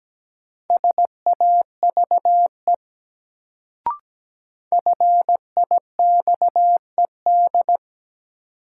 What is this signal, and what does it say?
Morse code, heard as a keyed tone, about 17 wpm, 700 Hz. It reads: SAVE FIXED
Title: Sentences from Top 1100 Words - New Word Drill Alt - 17wpm